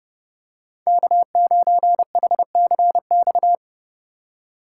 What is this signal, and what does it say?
Morse code, heard as a keyed tone, about 32 wpm, 700 Hz. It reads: K9HCX
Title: USA Call Signs - 5 Characters - 3x Character Spacing - 30wpm